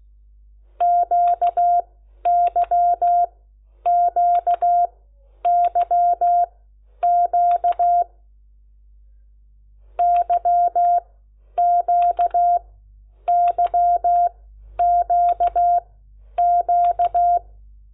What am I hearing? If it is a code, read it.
Morse code, heard as a keyed tone, about 16 wpm, spaced as for 10 wpm, 700 Hz. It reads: QYQYQ YQYQQ
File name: Lecciones / CLASE Nº10